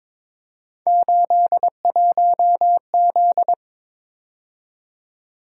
Morse code, heard as a keyed tone, about 22 wpm, 700 Hz. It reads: 81Z
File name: practice / 22wpm / Three Letter-Number Combinations - 22wpm